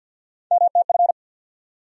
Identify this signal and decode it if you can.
Morse code, heard as a keyed tone, about 53 wpm, 700 Hz. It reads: MTF
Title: USA Call Sign Postfixes - 3 Letters 50wpm